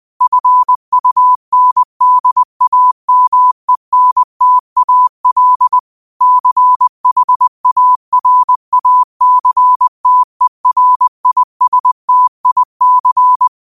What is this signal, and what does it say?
Morse code, heard as a keyed tone, about 20 wpm, 1000 Hz. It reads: FUNDAMENTAL CHARACTERISTIC